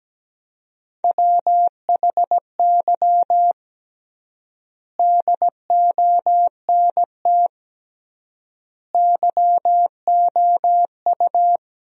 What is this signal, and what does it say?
Morse code, heard as a keyed tone, about 17 wpm, 700 Hz. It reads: WHY DONT YOU
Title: 3x Word-Spacing - Sentences from Top 300 Words - 17wpm